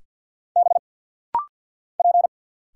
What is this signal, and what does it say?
Morse code, heard as a keyed tone, about 50 wpm, 700 Hz. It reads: B P
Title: Single Letters - Rapid-Fire 50wpm